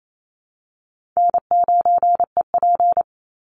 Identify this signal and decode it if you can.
Morse code, heard as a keyed tone, about 28 wpm, 700 Hz. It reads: N9EP
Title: Easy Call Signs - 28wpm